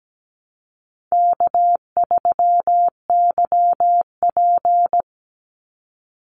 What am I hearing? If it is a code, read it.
Morse code, heard as a keyed tone, about 17 wpm, 700 Hz. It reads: K3YP
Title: Call Signs - 17wpm